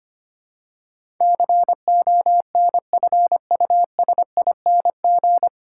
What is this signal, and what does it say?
Morse code, heard as a keyed tone, about 25 wpm, 690 Hz. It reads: CONFUSING
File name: practice / 25wpm / 9 Letter Words - First 3 Letters Given - 25wpm